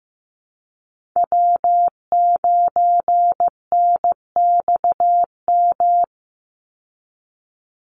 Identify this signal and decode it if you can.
Morse code, heard as a keyed tone, about 15 wpm, 700 Hz. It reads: W9NXM
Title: Call Signs Repeated 3x - 15wpm